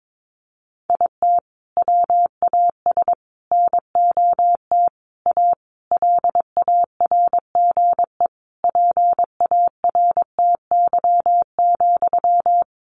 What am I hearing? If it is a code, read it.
Morse code, heard as a keyed tone, about 22 wpm, 700 Hz. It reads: IT WAS NOT A LARGE PARTY,
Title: Sentences from Top 1400 Words 22wpm